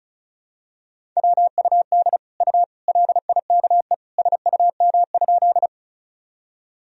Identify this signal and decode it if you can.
Morse code, heard as a keyed tone, about 36 wpm, 700 Hz. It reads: WUD U LIKE SUM?
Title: Sentences from Top 100 Words and Abbreviations 35wpm